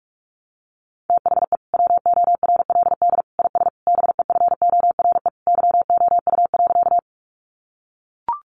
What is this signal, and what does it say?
Morse code, heard as a keyed tone, about 44 wpm, 700 Hz. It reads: THE WORLD IS BEFORE YOU.
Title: Sentences from Top 200 Words 45wpm